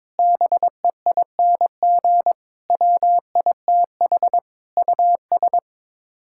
Morse code, heard as a keyed tone, about 22 wpm, 700 Hz. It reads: BEING WITH US
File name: Sets of 9 Words - 22wpm